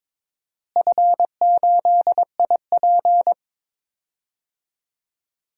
Morse code, heard as a keyed tone, about 22 wpm, 700 Hz. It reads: F8IP
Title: DXCC-100 Call Signs - 4 Characters - Repeated 2x - 22wpm